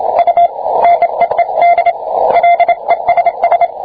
Morse code, heard as a keyed tone, about 27 wpm, 690 Hz, with radio noise. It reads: U NEED LESS